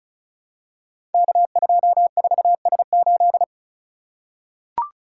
Morse code, heard as a keyed tone, about 36 wpm, 700 Hz. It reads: K24S8
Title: Code Groups - 5 Letter:Numbers - 35wpm